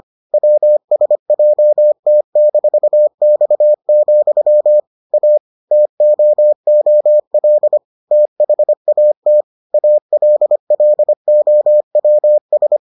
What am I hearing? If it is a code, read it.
Morse code, heard as a keyed tone, about 25 wpm, 600 Hz. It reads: WSJT-X, A TOOL THAT ALLOWS